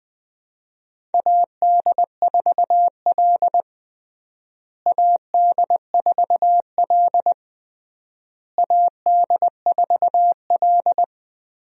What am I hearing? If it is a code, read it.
Morse code, heard as a keyed tone, about 20 wpm, 700 Hz. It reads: AD4L AD4L AD4L
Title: USA Call Signs - 4 Characters - Repeated 3x - 20wpm